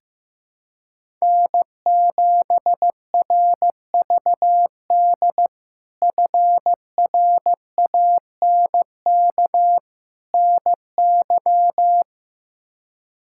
Morse code, heard as a keyed tone, about 15 wpm, 700 Hz. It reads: N7RVD FRANK NY
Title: K1USN SST QSO Practice - 15wpm